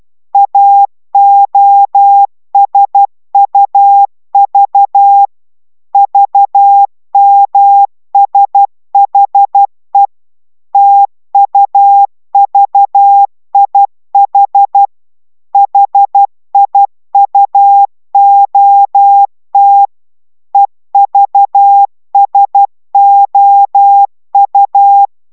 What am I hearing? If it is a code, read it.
Morse code, heard as a keyed tone, about 12 wpm, 800 Hz. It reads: AOSUV VMSHE TUVIH HIUOT EVSOU